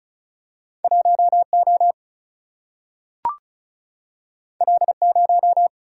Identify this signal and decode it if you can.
Morse code, heard as a keyed tone, about 36 wpm, 700 Hz. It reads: 1O L0
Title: Single-Letter Single-Number 35wpm